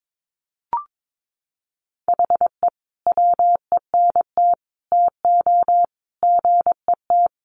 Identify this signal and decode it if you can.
Morse code, heard as a keyed tone, about 22 wpm, 700 Hz. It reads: HE WENT TO GET
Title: Sentences from Top 1500 Words 22wpm